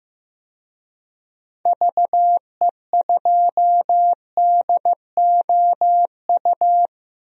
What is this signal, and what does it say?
Morse code, heard as a keyed tone, about 15 wpm, 700 Hz. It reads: VE2DOU
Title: DXCC-100 Call Signs - 6 Characters - 15wpm